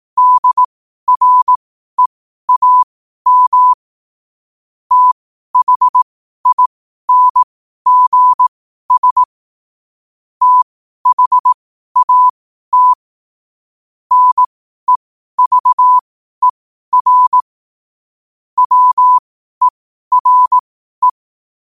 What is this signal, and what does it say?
Morse code, heard as a keyed tone, about 18 wpm, spaced as for 12 wpm, 1000 Hz. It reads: DREAM THINGS THAT NEVER WERE
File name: qotdfiles / 12wpm